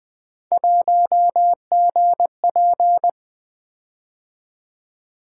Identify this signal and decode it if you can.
Morse code, heard as a keyed tone, about 20 wpm, 700 Hz. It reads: 1GP
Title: Three Letter-Number Combinations - 20wpm